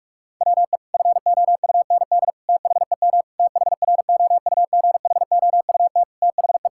Morse code, heard as a keyed tone, about 41 wpm, 700 Hz. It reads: WE FOUND THEM THROUGHOUT THE
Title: Sentences from Top 1000 Words - Review Part 5 - 45wpm